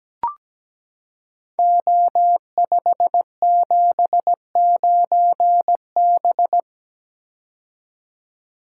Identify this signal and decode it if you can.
Morse code, heard as a keyed tone, about 17 wpm, 700 Hz. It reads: O579B
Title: Code Groups - 5 Letter:Numbers - 17wpm